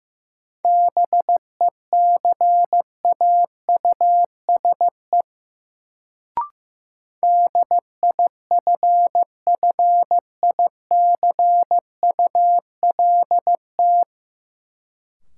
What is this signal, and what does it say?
Morse code, heard as a keyed tone, about 16 wpm, 700 Hz. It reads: BECAUSE DIFFICULT
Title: Sentences from Top 600 Words - New Word Drill - 15wpm